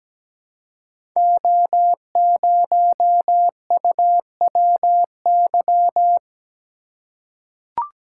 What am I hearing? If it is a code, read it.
Morse code, heard as a keyed tone, about 17 wpm, 700 Hz. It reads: O0UWY
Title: Code Groups - 5 Letter:Numbers - 17wpm